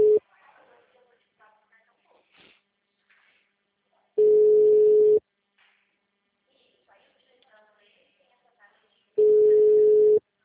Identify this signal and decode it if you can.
Ringback tone (Europe, 425 Hz), with speech.
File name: out- -2017- - - .46938